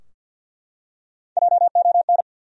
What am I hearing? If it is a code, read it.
Morse code, heard as a keyed tone, about 50 wpm, 700 Hz. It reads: JON
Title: USA Common Names - 50wpm